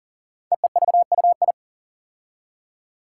Morse code, heard as a keyed tone, about 40 wpm, 700 Hz. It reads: EEVUI
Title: Code Groups - 5 Letters - 40wpm